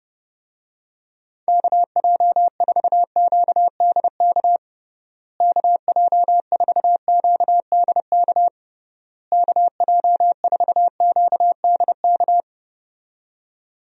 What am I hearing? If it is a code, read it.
Morse code, heard as a keyed tone, about 30 wpm, 700 Hz. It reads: KJ4QDK KJ4QDK KJ4QDK
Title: USA Call Signs - 6 Characters - Repeated 3x - 30wpm